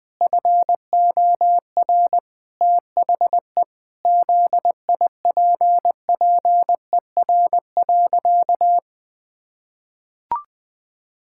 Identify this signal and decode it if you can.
Morse code, heard as a keyed tone, about 21 wpm, 700 Hz. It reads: FOR THE ZIPPER.
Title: Pangram Challenge v2 - 20wpm